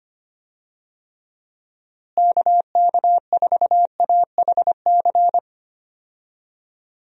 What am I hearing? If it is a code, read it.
Morse code, heard as a keyed tone, about 25 wpm, 700 Hz. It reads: KK4AHC
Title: USA Call Signs - 6 Characters - Repeated 3x - 25wpm